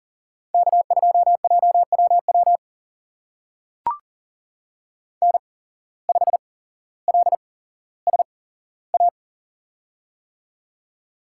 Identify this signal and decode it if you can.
Morse code, heard as a keyed tone, about 41 wpm, spaced as for 17 wpm, 700 Hz. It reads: K2JWW N5LSA
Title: USA Call Signs - 5 Characters - 8x Character Spacing - 40wpm